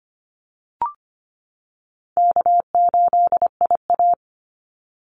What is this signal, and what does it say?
Morse code, heard as a keyed tone, about 25 wpm, 700 Hz. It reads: K8IA
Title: Call Signs - 25wpm